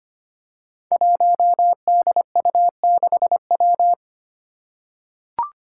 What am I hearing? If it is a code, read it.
Morse code, heard as a keyed tone, about 25 wpm, 700 Hz. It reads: 1DU6W
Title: Five Challenging Letter-Number Combinations 25wpm